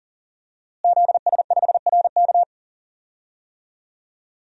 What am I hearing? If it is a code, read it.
Morse code, heard as a keyed tone, about 40 wpm, 700 Hz. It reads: ZS5RK
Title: DXCC-100 Call Signs - 5 Characters - Repeated 2x - 40wpm